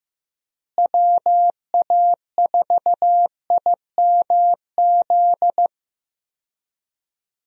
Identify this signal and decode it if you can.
Morse code, heard as a keyed tone, about 16 wpm, 700 Hz. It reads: WA4IMZ
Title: USA Call Signs - 6 Characters - Repeated 3x - 15wpm